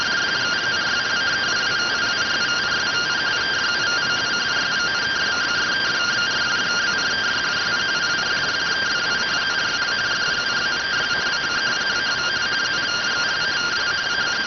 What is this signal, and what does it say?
Two-tone teleprinter signal, about 170 Hz shift.